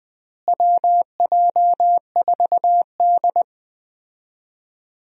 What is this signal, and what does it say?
Morse code, heard as a keyed tone, about 20 wpm, 700 Hz. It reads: WJ4D
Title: USA Call Signs - 4 Characters - Repeated 3x - 20wpm